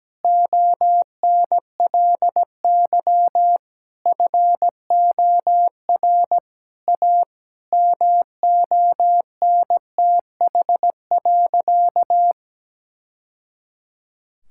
Morse code, heard as a keyed tone, about 17 wpm, 700 Hz. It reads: ONLY FOR A MONTH.
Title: Sentences from Top 1000 Words - Review Part 5 - 17wpm